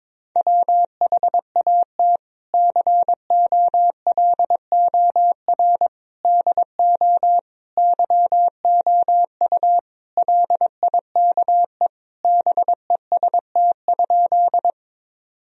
Morse code, heard as a keyed tone, about 22 wpm, 700 Hz. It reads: WHAT COLOR DO YOU LIKE BEST?
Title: Sentences from Top 1000 Words - Review Part 2 - 22wpm